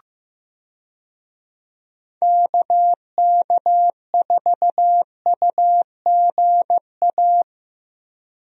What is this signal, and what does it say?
Morse code, heard as a keyed tone, about 15 wpm, 700 Hz. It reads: KK4UGA
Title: USA Call Signs - 6 Characters - Repeated 3x - 15wpm